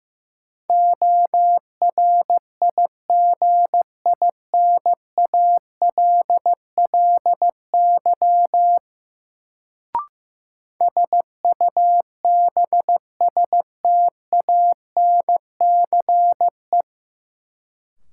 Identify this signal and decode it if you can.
Morse code, heard as a keyed tone, about 15 wpm, 700 Hz. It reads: ORIGINALLY SUBSTANCE
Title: Sentences from Top 1500 Words - New Word Drill - 15wpm